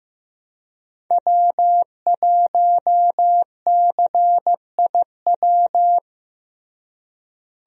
Morse code, heard as a keyed tone, about 15 wpm, 700 Hz. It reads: W1CIW
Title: USA Call Signs - 5 Characters - Repeated 2x - 15wpm